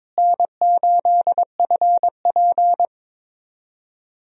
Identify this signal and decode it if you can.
Morse code, heard as a keyed tone, about 22 wpm, 700 Hz. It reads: N8FP
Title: USA Call Signs - 4 Characters - Repeated 3x - 22wpm